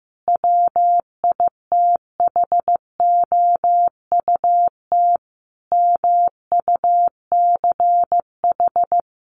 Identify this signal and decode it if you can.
Morse code, heard as a keyed tone, about 15 wpm, 700 Hz. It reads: WITHOUT MUCH
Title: Sentences from Top 900 Words 15wpm